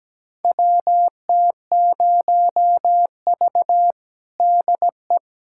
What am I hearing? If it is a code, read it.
Morse code, heard as a keyed tone, about 17 wpm, 700 Hz. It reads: WT0V DE
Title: USA Callsign DE Callsign - 4 Characters - 17wpm